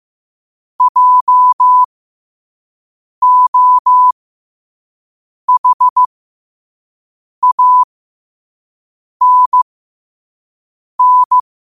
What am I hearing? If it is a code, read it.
Morse code, heard as a keyed tone, about 15 wpm, spaced as for 5 wpm, 1000 Hz. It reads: JOHANN
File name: qotdfiles / 5wpm